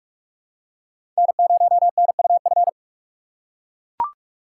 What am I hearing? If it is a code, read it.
Morse code, heard as a keyed tone, about 43 wpm, 700 Hz. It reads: N0NUF
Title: USA Call Signs - 5 Characters - 6x Character Spacing - 45wpm